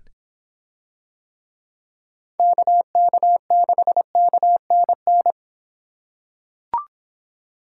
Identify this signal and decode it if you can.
Morse code, heard as a keyed tone, about 24 wpm, 700 Hz. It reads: KK6KNN